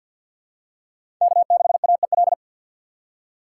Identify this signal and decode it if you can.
Morse code, heard as a keyed tone, about 52 wpm, 700 Hz. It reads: K6AEL
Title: USA Call Signs - 5 Characters - 2x Character Spacing - 50wpm